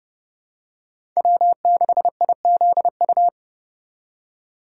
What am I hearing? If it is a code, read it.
Morse code, heard as a keyed tone, about 30 wpm, 700 Hz. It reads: W6IZU
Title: USA Call Signs - 4x Character Spacing - 30wpm